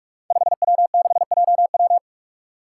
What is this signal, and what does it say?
Morse code, heard as a keyed tone, about 43 wpm, 700 Hz. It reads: 5W6JW
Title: Five Challenging Letter-Number Combinations 45wpm